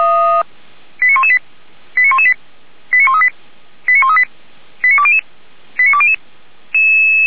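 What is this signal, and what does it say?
Five-tone selective call (ZVEI), heard as quick runs of five tones, tones of about 70 ms. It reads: 89118 89118 89138 89138 89200 89200
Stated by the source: ZVEI five-tone call received via SDR.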